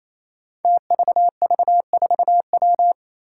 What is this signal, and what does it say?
Morse code, heard as a keyed tone, about 28 wpm, 700 Hz. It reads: TVV4W
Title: Five Challenging Letter-Number Combinations 28wpm